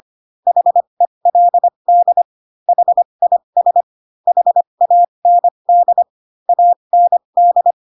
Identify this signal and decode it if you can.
Morse code, heard as a keyed tone, about 25 wpm, spaced as for 22 wpm, 700 Hz. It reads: HELD HIS HAND AND
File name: Pinocchio / Pinocchio ch3 18 25 ik1hge